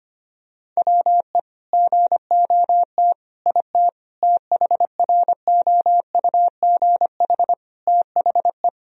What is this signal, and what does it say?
Morse code, heard as a keyed tone, about 25 wpm, 700 Hz. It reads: WE GOT IT THROUGH THE
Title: Sentences from Top 1000 Words - Review Part 1 - 25wpm